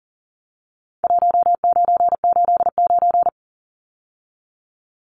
Morse code, heard as a keyed tone, about 40 wpm, 700 Hz. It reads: 1989
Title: Numbers - 4-Digit 40wpm